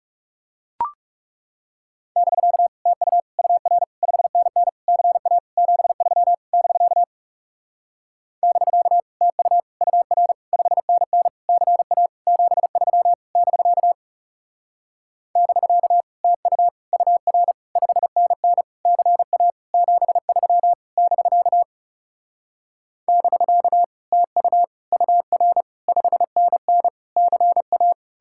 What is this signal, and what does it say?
Morse code, heard as a keyed tone, about 38 wpm, 700 Hz. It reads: <BK> TU UR 5NN CA 73 <BK> <BK> TU UR 5NN CA 73 <BK> <BK> TU UR 5NN CA 73 <BK> <BK> TU UR 5NN CA